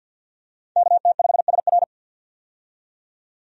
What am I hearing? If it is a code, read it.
Morse code, heard as a keyed tone, about 52 wpm, 700 Hz. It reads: KT5SR
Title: USA Call Signs - 5 Characters - 50wpm